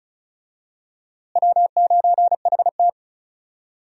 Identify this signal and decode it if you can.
Morse code, heard as a keyed tone, about 36 wpm, 700 Hz. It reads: W9HT
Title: USA Call Signs - 4 Characters - Repeated 3x - 35wpm